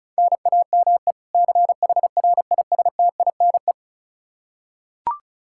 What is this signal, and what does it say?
Morse code, heard as a keyed tone, about 36 wpm, 700 Hz. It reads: NAME CHRISTINE
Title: Practice QSO - Exchange 1 - Beginning - 35wpm